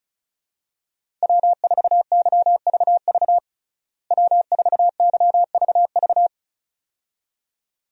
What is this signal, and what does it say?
Morse code, heard as a keyed tone, about 36 wpm, 700 Hz. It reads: W4YVV W4YVV
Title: USA Call Signs - 5 Characters - Repeated 2x - 35wpm